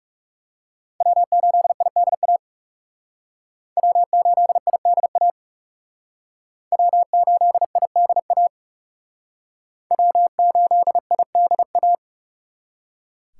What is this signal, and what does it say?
Morse code, heard as a keyed tone, about 36 wpm, 700 Hz. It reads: W8IDA W8IDA W8IDA W8IDA
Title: Speed Racing - USA 1x3 Call Signs 30wpm